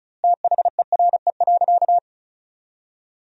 Morse code, heard as a keyed tone, about 34 wpm, 700 Hz. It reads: THERE.
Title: 2x Word-Spacing - Sentences from Top 500 Words - 35wpm